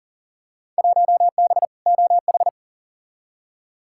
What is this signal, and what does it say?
Morse code, heard as a keyed tone, about 40 wpm, 700 Hz. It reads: 1B OH
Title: Field Day Exchanges - 40wpm